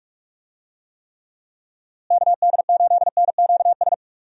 Morse code, heard as a keyed tone, about 43 wpm, 700 Hz. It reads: KD8NQS
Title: USA Call Signs - 6 Characters - Repeated 3x - 45wpm